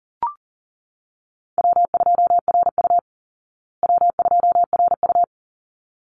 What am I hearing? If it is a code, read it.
Morse code, heard as a keyed tone, about 40 wpm, 700 Hz. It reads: W2RU W2RU
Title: Call Signs Repeated 3x - 40wpm